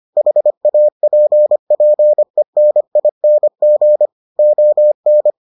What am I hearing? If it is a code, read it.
Morse code, heard as a keyed tone, about 25 wpm, 600 Hz. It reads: HAPPENING ON